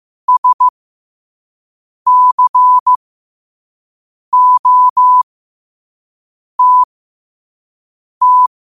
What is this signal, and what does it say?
Morse code, heard as a keyed tone, about 15 wpm, spaced as for 5 wpm, 1000 Hz. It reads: SCOTT
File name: qotdfiles / 5wpm